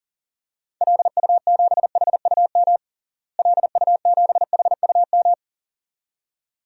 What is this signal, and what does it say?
Morse code, heard as a keyed tone, about 37 wpm, 700 Hz. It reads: LU7HUM LU7HUM
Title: DXCC-100 Call Signs - 6 Characters - Repeated 2x - 40wpm